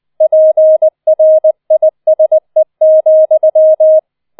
Morse code, heard as a keyed tone, about 20 wpm, 630 Hz. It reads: PRISE,
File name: 20wpm